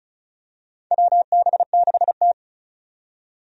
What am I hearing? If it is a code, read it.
Morse code, heard as a keyed tone, about 36 wpm, 700 Hz. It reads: WB6T
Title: USA Call Signs - 4 Characters - 35wpm